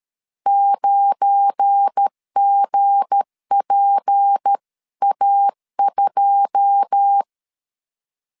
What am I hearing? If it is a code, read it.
Morse code, heard as a keyed tone, about 13 wpm, 800 Hz. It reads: 9GPA2